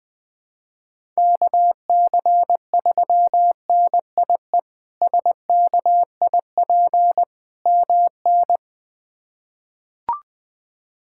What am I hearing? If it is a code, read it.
Morse code, heard as a keyed tone, about 20 wpm, 700 Hz. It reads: KC3NIE SKIP MN
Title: K1USN SST QSO Practice - 20wpm